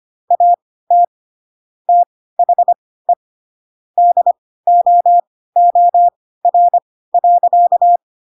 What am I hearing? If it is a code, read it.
Morse code, heard as a keyed tone, about 25 wpm, spaced as for 16 wpm, 700 Hz. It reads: AT THE DOOR.
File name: Pinocchio / Pinocchio ch6 10 25 ik1hge